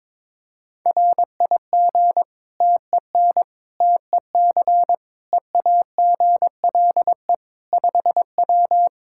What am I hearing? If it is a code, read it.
Morse code, heard as a keyed tone, about 22 wpm, 700 Hz. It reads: RIG TEN TEC EAGLE 5W
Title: Practice QSO - Exchange 2 - Info - 22wpm